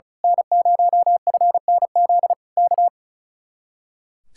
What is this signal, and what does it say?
Morse code, heard as a keyed tone, about 35 wpm, 700 Hz. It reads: N0FNZ K
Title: Speed-Racing Common QSO Phrases - 35wpm